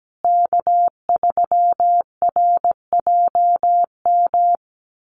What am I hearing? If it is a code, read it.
Morse code, heard as a keyed tone, about 17 wpm, 700 Hz. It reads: K3RJM
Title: USA Call Signs Repeated 2x 17wpm